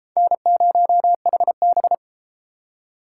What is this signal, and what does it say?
Morse code, heard as a keyed tone, about 32 wpm, 700 Hz. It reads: N0HB